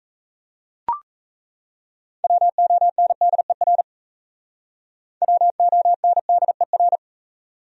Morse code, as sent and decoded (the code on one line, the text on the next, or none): .-- --- -. -.. .-. / .-- --- -. -.. .-.
WONDER WONDER